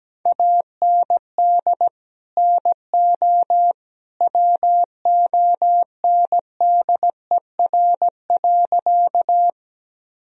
Morse code, heard as a keyed tone, about 17 wpm, 700 Hz. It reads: AND NO WONDER.
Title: Sentences from Top 1000 Words - Review Part 3 - 17wpm